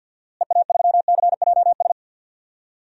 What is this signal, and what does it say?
Morse code, heard as a keyed tone, about 53 wpm, 700 Hz. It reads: EA3CJS
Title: DXCC-100 Call Signs - 6 Characters - Repeated 2x - 50wpm